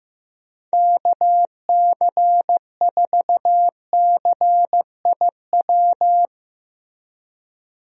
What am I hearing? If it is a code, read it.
Morse code, heard as a keyed tone, about 16 wpm, 700 Hz. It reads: KC4CIW
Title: USA Call Signs - 6 Characters - Repeated 2x - 15wpm